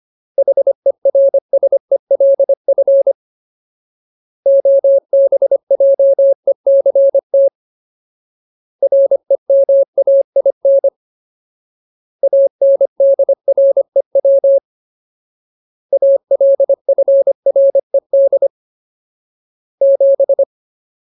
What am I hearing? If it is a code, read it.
Morse code, heard as a keyed tone, about 25 wpm, 550 Hz. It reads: HERSELF OBJECT REMAIN ANDREW ALFRED 7
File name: Ultimate Word List / 25WPM Ultimate